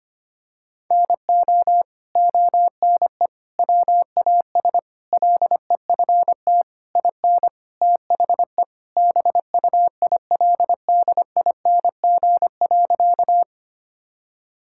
Morse code, heard as a keyed tone, about 25 wpm, 700 Hz. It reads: NO ONE WAS LEFT IN THE BUILDING.
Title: Sentences from Top 1000 Words - Review Part 5 - 25wpm